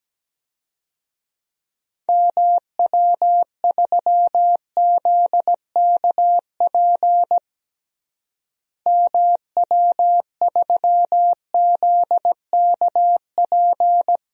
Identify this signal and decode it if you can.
Morse code, heard as a keyed tone, about 17 wpm, 700 Hz. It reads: MW3ZKP MW3ZKP
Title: DXCC-100 Call Signs - 6 Characters - Repeated 2x - 17wpm